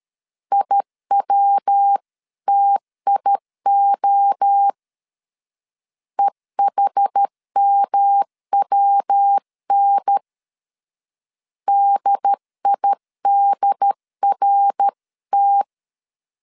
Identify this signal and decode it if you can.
Morse code, heard as a keyed tone, about 13 wpm, 800 Hz. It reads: IWTIO EHMWN DIDRT
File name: CW / DET / 1 3 1500C S A 12wpm 2905